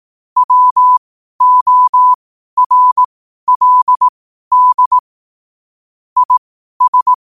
Morse code, heard as a keyed tone, about 18 wpm, spaced as for 12 wpm, 1000 Hz. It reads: WORLD IS
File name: qotdfiles / 12wpm